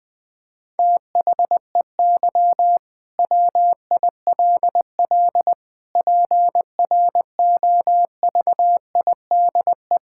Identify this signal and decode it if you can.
Morse code, heard as a keyed tone, about 20 wpm, 700 Hz. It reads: THEY WILL PROVIDE
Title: Sentences from Top 1000 Words - Review Part 5 - 20wpm